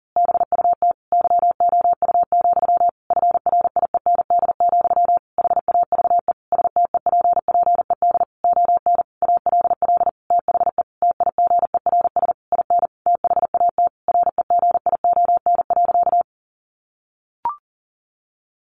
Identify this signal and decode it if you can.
Morse code, heard as a keyed tone, about 40 wpm, 700 Hz. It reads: BUT YOU, FRIEND, HAVE STEPPED ON ALL THE TIGERS IN THAT REGION.
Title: A Sound Of Thunder 40wpm